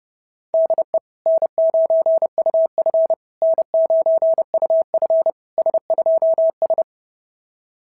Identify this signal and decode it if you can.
Morse code, heard as a keyed tone, about 30 wpm, 660 Hz. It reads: DE N9UF N9UF S2S
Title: SOTA QSO Practice 30wpm